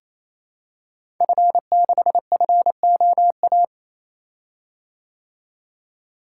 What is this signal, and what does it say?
Morse code, heard as a keyed tone, about 28 wpm, 700 Hz. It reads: F6FOA
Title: Code Groups - 5 Letter:Numbers - 28wpm